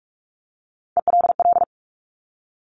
Morse code, heard as a keyed tone, about 46 wpm, 700 Hz. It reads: ELL
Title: Top Three Letter Combinations in Words 45wpm